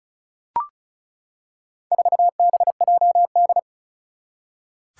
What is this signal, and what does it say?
Morse code, heard as a keyed tone, about 35 wpm, 700 Hz. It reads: 4BJD
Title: Four Challenging Letter-Number Combinations 35wpm